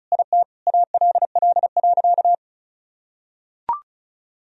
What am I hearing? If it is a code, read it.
Morse code, heard as a keyed tone, about 34 wpm, 700 Hz. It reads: IT ALL.
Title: Sentences from Top 1000 Words - Review Part 1 - 35wpm